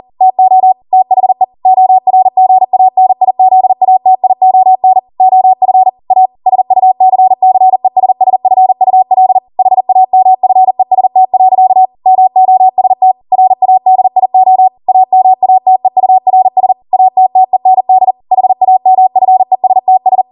Morse code, heard as a keyed tone, about 40 wpm, 750 Hz. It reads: TO THE ORGANIZATION OF A SUCCESSFUL HAMFEST. MOST RADIO AMATEURS ATTEND HAMFESTS